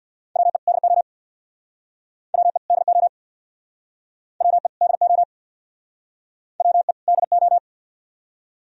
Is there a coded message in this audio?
Yes: Morse code from 0.4 to 7.6 s, heard as a keyed tone, about 67 wpm, 700 Hz. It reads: WE DO WE DO WE DO WE DO